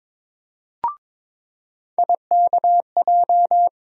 Morse code, heard as a keyed tone, about 22 wpm, 700 Hz. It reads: IKJ
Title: USA Call Sign Postfixes - 3 Letters 22wpm